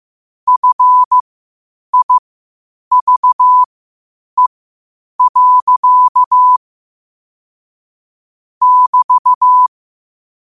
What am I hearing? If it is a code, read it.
Morse code, heard as a keyed tone, about 15 wpm, spaced as for 8 wpm, 1000 Hz. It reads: FIVE. <BT>
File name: qotdfiles / 8wpm